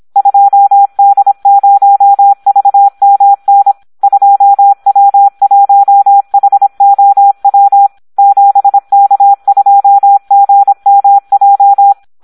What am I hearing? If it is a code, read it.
Morse code, heard as a keyed tone, about 26 wpm, 800 Hz, with radio noise. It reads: 2D0VMN 2W1HOW 7K2GMJ
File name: Audio practica CW / SD 1 26